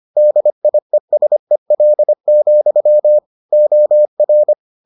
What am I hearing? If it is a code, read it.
Morse code, heard as a keyed tone, about 25 wpm, 600 Hz. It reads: DIESEL, OR